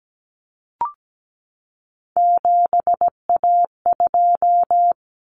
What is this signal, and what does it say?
Morse code, heard as a keyed tone, about 17 wpm, 690 Hz. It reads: 7A2
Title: Common Cut Numbers - 3-Digit 17wpm